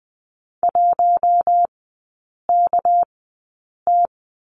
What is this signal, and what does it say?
Morse code, heard as a keyed tone, about 20 wpm, spaced as for 8 wpm, 690 Hz. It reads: 1KT
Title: 198 Lesson 32e - Letter k - Three Characters 20wpm